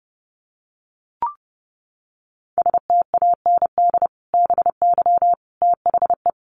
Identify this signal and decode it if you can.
Morse code, heard as a keyed tone, about 30 wpm, 700 Hz. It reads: STAND BY THE
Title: Sentences from Top 1200 Words 30wpm